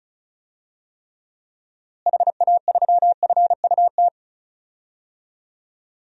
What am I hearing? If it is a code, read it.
Morse code, heard as a keyed tone, about 34 wpm, 700 Hz. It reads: HA3FUT
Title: DXCC-100 Call Signs - 6 Characters - Repeated 2x - 35wpm